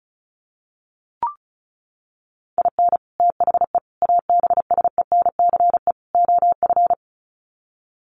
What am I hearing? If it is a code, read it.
Morse code, heard as a keyed tone, about 34 wpm, 700 Hz. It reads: IN THE ABSENCE OF
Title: Sets of 4 Words 35wpm